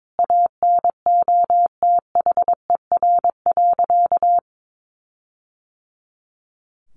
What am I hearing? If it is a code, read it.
Morse code, heard as a keyed tone, about 22 wpm, 700 Hz. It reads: ANOTHER.
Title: Sentences from Top 1400 Words 22wpm